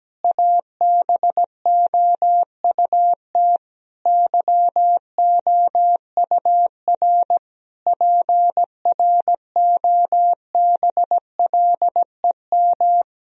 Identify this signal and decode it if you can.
Morse code, heard as a keyed tone, about 17 wpm, 700 Hz. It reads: ABOUT YOUR PROBLEM
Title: Sentences from Top 1000 Words - Review Part 3 - 17wpm